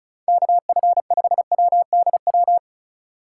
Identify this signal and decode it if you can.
Morse code, heard as a keyed tone, about 44 wpm, 700 Hz. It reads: KF5WDW
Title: Speed Racing - USA 2x3 Call Signs 35wpm